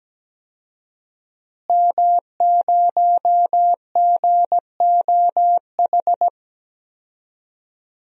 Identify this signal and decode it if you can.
Morse code, heard as a keyed tone, about 17 wpm, 700 Hz. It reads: M0GOH
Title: DXCC-100 Call Signs - 5 Characters - Repeated 2x - 17wpm